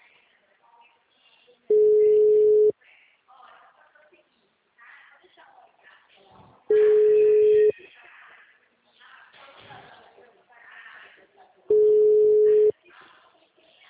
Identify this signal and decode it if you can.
Ringback tone (Europe, 425 Hz), with speech.